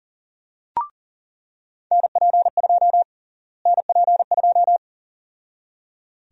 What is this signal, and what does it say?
Morse code, heard as a keyed tone, about 38 wpm, 700 Hz. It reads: NP2 NP2
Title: USA Call Sign Prefixes - Repeated 2x - 40wpm